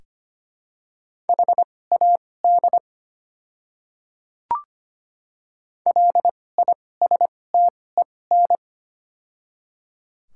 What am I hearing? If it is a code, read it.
Morse code, heard as a keyed tone, about 25 wpm, spaced as for 18 wpm, 700 Hz. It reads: HAD LISTEN
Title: 2x Character Spacing - Top 500 Words - 25wpm